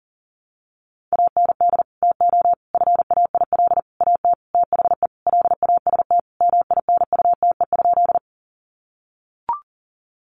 Morse code, heard as a keyed tone, about 40 wpm, 700 Hz. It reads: AND TO FAIL AT THE LAST MINUTE?
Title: Sentences from Top 1100 Words 40wpm